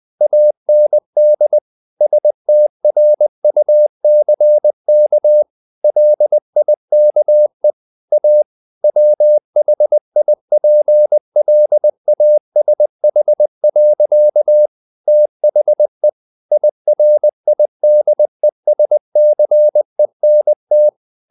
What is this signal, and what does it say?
Morse code, heard as a keyed tone, about 20 wpm, 600 Hz. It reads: AND STRUCK LIKE A WHIPLASH. THE IRIDESCENT
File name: the old man and the sea / the old man and the sea part 1 20wpm